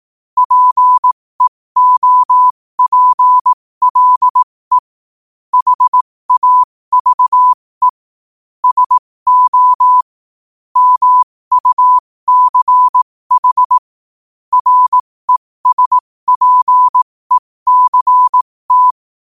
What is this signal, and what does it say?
Morse code, heard as a keyed tone, about 18 wpm, spaced as for 15 wpm, 1000 Hz. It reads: PEOPLE HAVE SO MUCH RESPECT
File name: qotdfiles / 15wpm